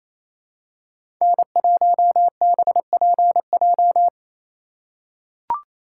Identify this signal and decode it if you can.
Morse code, heard as a keyed tone, about 28 wpm, 700 Hz. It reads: N1BPJ
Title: USA Call Signs - 2x Character Spacing - 28wpm